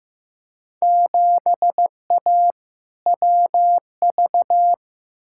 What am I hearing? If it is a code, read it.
Morse code, heard as a keyed tone, about 15 wpm, 700 Hz. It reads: 7A WV
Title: Field Day Exchanges Encore - 15wpm